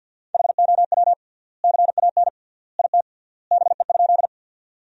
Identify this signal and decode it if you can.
Morse code, heard as a keyed tone, about 52 wpm, 700 Hz. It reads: HOW CAN IT BE?
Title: 3x Word-Spacing - Sentences from Top 100 Words - 50wpm